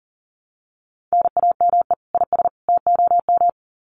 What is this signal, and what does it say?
Morse code, heard as a keyed tone, about 40 wpm, 700 Hz. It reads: NAME IS TOM
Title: Common QSO Phrases 40wpm